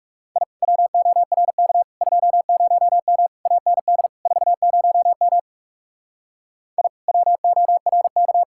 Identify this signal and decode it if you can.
Morse code, heard as a keyed tone, about 37 wpm, 700 Hz. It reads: I WORK 20M AND 40M I WORK
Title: Speed-Racing Common QSO Phrases Encore - 30wpm